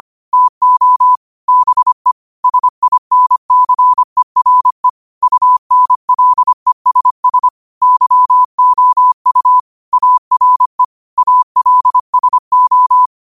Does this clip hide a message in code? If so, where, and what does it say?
Yes: Morse code from 0.3 to 13.1 s, heard as a keyed tone, about 25 wpm, 1000 Hz. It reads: TO BE SINCERE UNLESS YOU ARE ALSO